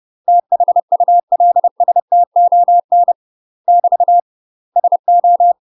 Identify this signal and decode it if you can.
Morse code, heard as a keyed tone, about 30 wpm, 700 Hz. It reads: THULSTON <BT> SO
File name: UK MASTER QSOs TRAINING / 30WPM / UK MASTER QSOs 0048